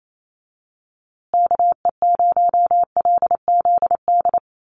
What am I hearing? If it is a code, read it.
Morse code, heard as a keyed tone, about 28 wpm, 700 Hz. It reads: KE0LZD
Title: USA Call Signs Repeated 2x 28wpm